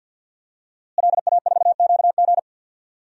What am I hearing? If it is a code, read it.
Morse code, heard as a keyed tone, about 47 wpm, 700 Hz. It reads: LA4QG
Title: DXCC-100 Call Signs - 5 Characters - Repeated 2x - 50wpm